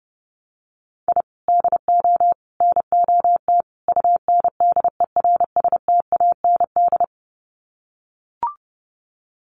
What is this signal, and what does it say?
Morse code, heard as a keyed tone, about 30 wpm, 700 Hz. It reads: I DO NOT UNDERSTAND
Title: Sets of 4 Words 30wpm